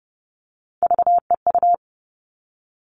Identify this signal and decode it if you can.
Morse code, heard as a keyed tone, about 30 wpm, 700 Hz. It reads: VEU
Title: All Cut Numbers - 3-Digit 30wpm